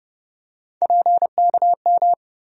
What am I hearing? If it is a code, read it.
Morse code, heard as a keyed tone, about 31 wpm, 700 Hz. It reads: PKM
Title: USA Call Sign Postfixes - 3 Letters 30wpm